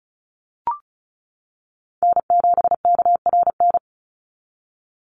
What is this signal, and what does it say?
Morse code, heard as a keyed tone, about 34 wpm, 700 Hz. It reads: N7KRN
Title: NA SOTA Chaser Callsigns 35wpm